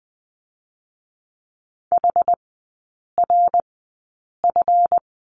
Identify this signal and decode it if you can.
Morse code, heard as a keyed tone, about 20 wpm, spaced as for 8 wpm, 690 Hz. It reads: HRF
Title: 212 Lesson 34e - Number 8 - Three Characters 20wpm